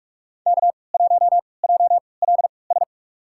Morse code, heard as a keyed tone, about 47 wpm, spaced as for 26 wpm, 700 Hz. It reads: K1JLS
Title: USA Call Signs - 5 Characters - 3x Character Spacing - 45wpm